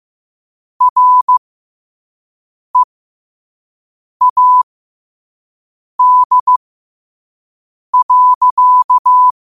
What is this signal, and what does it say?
Morse code, heard as a keyed tone, about 15 wpm, spaced as for 5 wpm, 1000 Hz. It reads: READ.
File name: qotdfiles / 5wpm